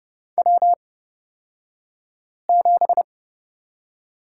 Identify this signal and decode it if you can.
Morse code, heard as a keyed tone, about 30 wpm, spaced as for 5 wpm, 700 Hz. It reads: W7
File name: Lesson 36d - BT - Two Characters 30wpm